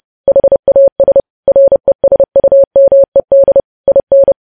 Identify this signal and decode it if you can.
Morse code, heard as a keyed tone, about 30 wpm, 550 Hz. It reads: HAS RESUMED IN
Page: news headlines in morse code at 30 wpm